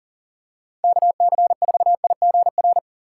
Morse code, heard as a keyed tone, about 40 wpm, 700 Hz. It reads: KC4IGR
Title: USA Call Signs - 6 Characters - 6x Character Spacing - 40wpm